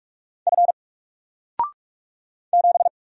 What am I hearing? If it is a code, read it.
Morse code, heard as a keyed tone, about 44 wpm, 700 Hz. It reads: F 7
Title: Single Letter-Number 45wpm